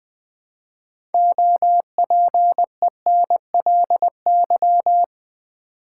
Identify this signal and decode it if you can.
Morse code, heard as a keyed tone, about 20 wpm, 700 Hz. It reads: OPENLY
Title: 6 Letter Words - First 3 Letters Given - 20wpm